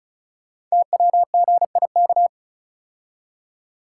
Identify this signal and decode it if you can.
Morse code, heard as a keyed tone, about 36 wpm, 700 Hz. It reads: TWGIK
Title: Code Groups - 5 Letters - 35wpm